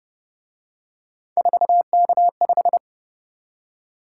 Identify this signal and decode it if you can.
Morse code, heard as a keyed tone, about 31 wpm, 700 Hz. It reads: 4K5
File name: Speed-Racing - Three Letter-Number Combinations - 25wpm